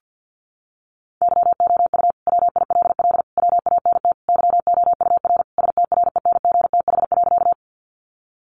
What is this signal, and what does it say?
Morse code, heard as a keyed tone, about 50 wpm, 700 Hz. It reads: YOU WILL WANT YOUR STRENGTH.
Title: Sentences from Top 1300 Words 50wpm